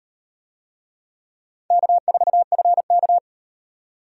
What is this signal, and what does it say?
Morse code, heard as a keyed tone, about 38 wpm, 700 Hz. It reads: K4FK